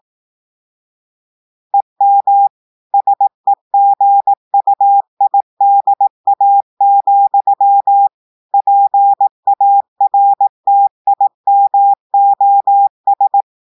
Morse code, heard as a keyed tone, about 18 wpm, 800 Hz. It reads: EM SEGUIDA, PARTIMOS
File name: audios / portugais / portugais texte en clair 2 vitesse 18 mots minutes